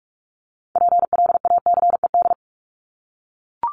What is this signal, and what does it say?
Morse code, heard as a keyed tone, about 44 wpm, 700 Hz. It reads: PLACED
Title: ED Suffix - Top 100 Words with Base Words 45wpm